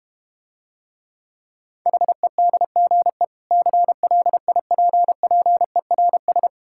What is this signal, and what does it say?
Morse code, heard as a keyed tone, about 31 wpm, 700 Hz. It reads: HEDGE CLIPPERS
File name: practice / 25wpm / Difficult Words - Speed-Racing with Hints - 25wpm